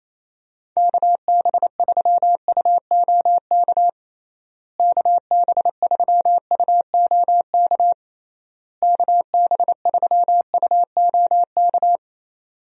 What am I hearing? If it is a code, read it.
Morse code, heard as a keyed tone, about 28 wpm, 700 Hz. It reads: KB3UOK KB3UOK KB3UOK
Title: USA Call Signs - 6 Characters - Repeated 3x - 28wpm